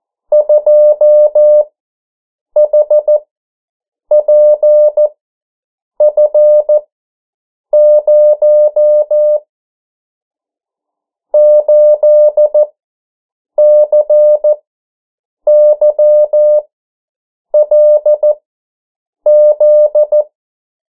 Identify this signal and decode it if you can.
Morse code, heard as a keyed tone, about 14 wpm, spaced as for 7 wpm, 600 Hz. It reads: 2HPF0 8CYLZ
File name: morse mp3 / 09 urok 8